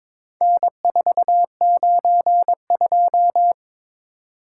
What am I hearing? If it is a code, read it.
Morse code, heard as a keyed tone, about 22 wpm, 700 Hz. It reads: N492
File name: practice / 22wpm / Four Letter-Number Combinations - 22wpm